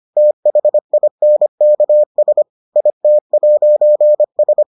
Morse code, heard as a keyed tone, about 25 wpm, 600 Hz. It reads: THINKS IT'S